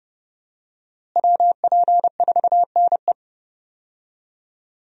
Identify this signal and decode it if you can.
Morse code, heard as a keyed tone, about 30 wpm, 700 Hz. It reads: WP4NE
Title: USA Call Signs - 5 Characters - Repeated 2x - 30wpm